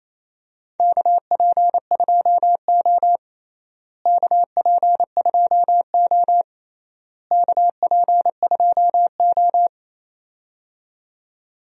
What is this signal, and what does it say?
Morse code, heard as a keyed tone, about 28 wpm, 700 Hz. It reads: KP2O KP2O KP2O
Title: DXCC-100 Call Signs - 4 Characters - Repeated 3x - 28wpm